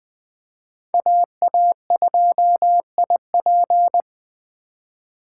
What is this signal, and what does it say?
Morse code, heard as a keyed tone, about 20 wpm, 700 Hz. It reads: AA2IP
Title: USA Call Signs - 5 Characters - 2x Character Spacing - 20wpm